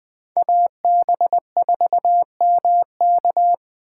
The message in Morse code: AB4MK

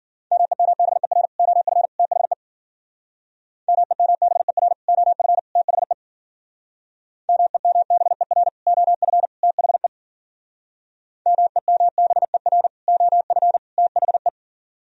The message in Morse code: MEMBER OF THE MEMBER OF THE MEMBER OF THE MEMBER OF THE

